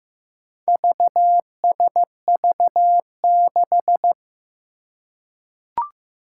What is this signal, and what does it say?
Morse code, heard as a keyed tone, about 15 wpm, 700 Hz. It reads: VSV6
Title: Four Challenging Letter-Number Combinations 15wpm